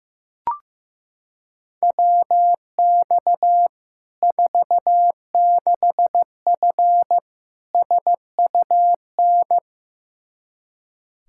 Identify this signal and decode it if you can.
Morse code, heard as a keyed tone, about 15 wpm, 700 Hz. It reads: WX 46F SUN
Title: Practice QSO - Exchange 2 - Info - 15wpm